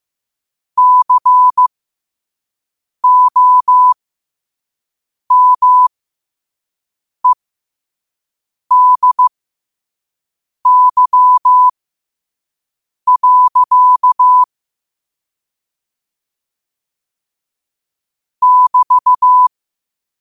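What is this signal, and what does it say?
Morse code, heard as a keyed tone, about 15 wpm, spaced as for 5 wpm, 1000 Hz. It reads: COMEDY. <BT>
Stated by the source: Morse quote of the day at 5 wpm.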